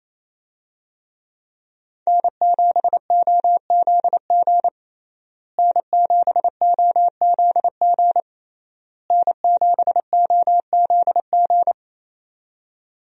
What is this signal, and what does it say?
Morse code, heard as a keyed tone, about 27 wpm, 700 Hz. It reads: N7OZG N7OZG N7OZG
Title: USA Call Signs - 5 Characters - Repeated 3x - 28wpm